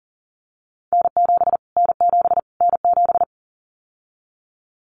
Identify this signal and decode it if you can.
Morse code, heard as a keyed tone, about 38 wpm, 700 Hz. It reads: N7 N7 N7
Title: Callsign Prefixes for Top 25 DXCC - Repeated 3x - 40wpm